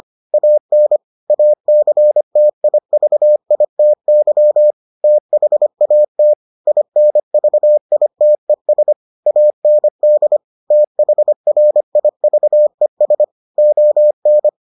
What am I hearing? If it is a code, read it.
Morse code, heard as a keyed tone, about 25 wpm, 600 Hz. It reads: AN ACTIVITY THAT INVITES AND THRIVES ON